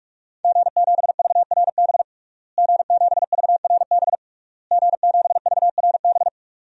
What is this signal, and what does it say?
Morse code, heard as a keyed tone, about 44 wpm, 700 Hz. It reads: G7VRB G7VRB G7VRB
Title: DXCC-100 Call Signs - 5 Characters - Repeated 3x - 45wpm